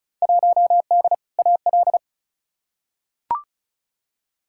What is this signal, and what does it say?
Morse code, heard as a keyed tone, about 36 wpm, 700 Hz. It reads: 1D AL
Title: Field Day Exchanges - 35wpm